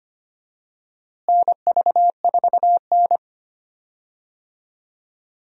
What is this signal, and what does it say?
Morse code, heard as a keyed tone, about 25 wpm, 700 Hz. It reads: NV4N
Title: USA Call Signs - 4 Characters - Repeated 2x - 25wpm